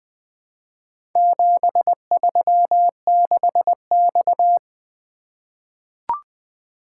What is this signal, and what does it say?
Morse code, heard as a keyed tone, about 20 wpm, 700 Hz. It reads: 736X
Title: Four Letter-Number Combinations - 20wpm